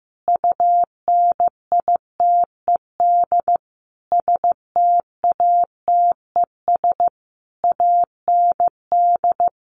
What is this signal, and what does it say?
Morse code, heard as a keyed tone, about 15 wpm, 700 Hz. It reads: UNITED STATES AND
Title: Sets of 5 Words Encore 15wpm